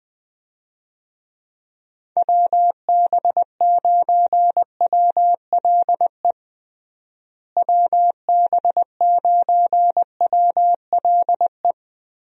Morse code, heard as a keyed tone, about 20 wpm, 700 Hz. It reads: WB9WLE WB9WLE
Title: USA Call Signs - 6 Characters - Repeated 3x - 20wpm